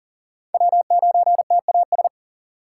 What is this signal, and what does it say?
Morse code, heard as a keyed tone, about 40 wpm, 700 Hz. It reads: W9TAS
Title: USA Call Signs - 5 Characters - 4x Character Spacing - 40wpm